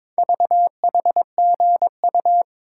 Morse code, heard as a keyed tone, about 23 wpm, 700 Hz. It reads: VHGU
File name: practice / 22wpm / Four Letter-Number Combinations - 22wpm